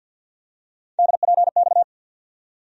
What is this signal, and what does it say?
Morse code, heard as a keyed tone, about 51 wpm, 700 Hz. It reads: DPX
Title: USA Call Sign Postfixes - 3 Letters 50wpm